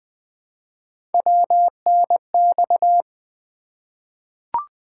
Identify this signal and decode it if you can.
Morse code, heard as a keyed tone, about 20 wpm, 700 Hz. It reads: WNX
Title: Three Letter-Number Combinations - 20wpm